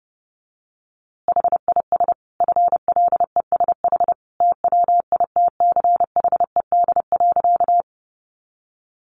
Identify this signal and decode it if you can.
Morse code, heard as a keyed tone, about 30 wpm, 700 Hz. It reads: HIS FLESH TWITCHED.
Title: A Sound Of Thunder 30wpm